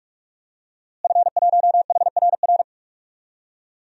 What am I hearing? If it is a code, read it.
Morse code, heard as a keyed tone, about 44 wpm, 700 Hz. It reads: F1HRR
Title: DXCC-100 Call Signs - 5 Characters - Repeated 3x - 45wpm